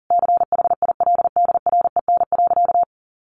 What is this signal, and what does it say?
Morse code, heard as a keyed tone, about 40 wpm, 700 Hz. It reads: CHILDREN.
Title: Sentences from Top 1500 Words 40wpm